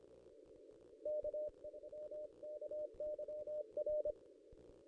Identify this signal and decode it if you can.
Morse code, heard as a keyed tone, about 25 wpm, 600 Hz. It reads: K MKYR
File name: contests / ss-cw-k5tr-op-n5rz -1500